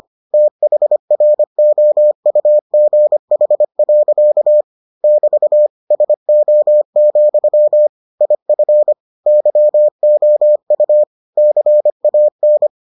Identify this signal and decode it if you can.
Morse code, heard as a keyed tone, about 25 wpm, 600 Hz. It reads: THROUGH. <BT> SO, IF YOU CAN